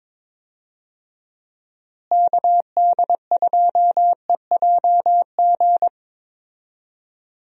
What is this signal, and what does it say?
Morse code, heard as a keyed tone, about 22 wpm, 700 Hz. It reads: KD2EJG
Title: USA Call Signs - 6 Characters - Repeated 3x - 22wpm